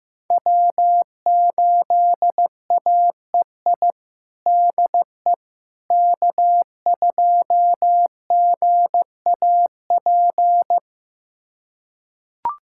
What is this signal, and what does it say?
Morse code, heard as a keyed tone, about 15 wpm, 700 Hz. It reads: W8AEI DE K2GAP
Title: USA Callsign DE Callsign - 5 Characters - 15wpm